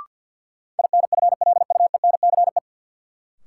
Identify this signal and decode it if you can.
Morse code, heard as a keyed tone, about 53 wpm, 700 Hz. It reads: INFLUENCE
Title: Sentences from Top 1100 Words - New Word Drill Alt - 50wpm